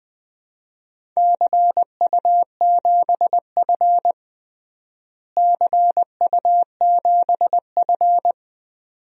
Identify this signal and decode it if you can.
Morse code, heard as a keyed tone, about 20 wpm, 700 Hz. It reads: CU7F CU7F
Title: DXCC-100 Call Signs - 4 Characters - Repeated 2x - 20wpm